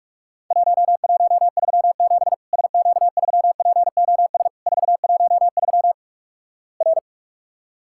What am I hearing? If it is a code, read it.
Morse code, heard as a keyed tone, about 46 wpm, 700 Hz. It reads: 1137 SQ3POS 413 R
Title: Worked All Europe DX Contest - QTC Exchange - 45wpm